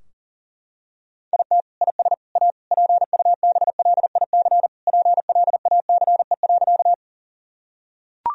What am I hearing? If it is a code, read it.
Morse code, heard as a keyed tone, about 37 wpm, 700 Hz. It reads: IT IS A PUBLIC PLACE.
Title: Sentences from Top 1000 Words - Review Part 3 - 40wpm